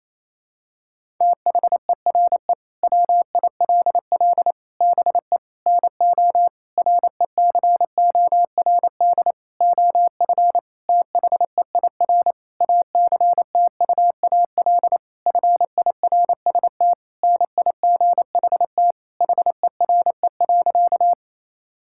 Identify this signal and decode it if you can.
Morse code, heard as a keyed tone, about 28 wpm, 700 Hz. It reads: THERE WILL BE NO RECORD OF THEIR ACTUAL FIRST NIGHT HERE.
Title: Sentences from Top 1000 Words - Review Part 5 - 28wpm